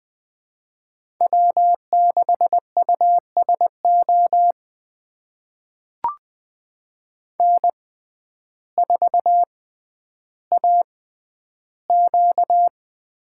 Morse code, heard as a keyed tone, about 21 wpm, spaced as for 11 wpm, 700 Hz. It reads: W6USO N4AQ
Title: USA Call Signs - 6x Character Spacing - 20wpm